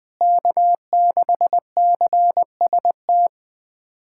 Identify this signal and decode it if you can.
Morse code, heard as a keyed tone, about 21 wpm, 700 Hz. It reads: K6CST
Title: USA Callsign DE Callsign - 5 Characters - 20wpm